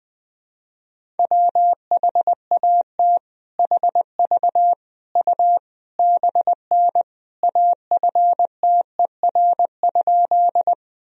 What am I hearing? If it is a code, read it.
Morse code, heard as a keyed tone, about 20 wpm, 700 Hz. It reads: WHAT HV U BN AFTER?
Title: Sentences from Top 200 Words and Abbreviations 20wpm